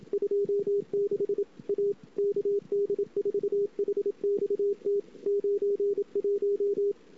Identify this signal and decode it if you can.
Morse code, heard as a keyed tone, about 28 wpm, 410 Hz. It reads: 26 A KD4HXT 91
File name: CW / 2008-ss-cw-k5tr-op-n5rz -22:00